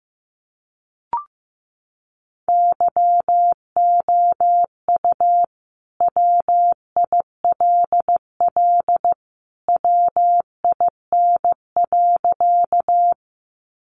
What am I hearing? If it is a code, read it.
Morse code, heard as a keyed tone, about 15 wpm, 700 Hz. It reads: YOU WILL WIN.
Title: Sentences from Top 1500 Words 15wpm